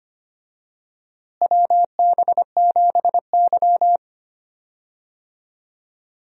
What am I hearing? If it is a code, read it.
Morse code, heard as a keyed tone, about 25 wpm, 700 Hz. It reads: WB7Y
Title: USA Call Signs - 4 Characters - Repeated 2x - 25wpm